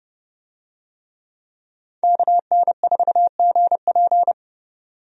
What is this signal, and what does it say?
Morse code, heard as a keyed tone, about 31 wpm, 700 Hz. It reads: KN4GP
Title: USA Call Signs - 5 Characters - 30wpm